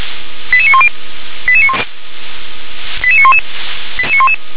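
Five-tone selective call (ZVEI), heard as quick runs of five tones, tones of about 70 ms. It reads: 88910 88911
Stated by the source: ZVEI five-tone selective call 88081 received on SDR.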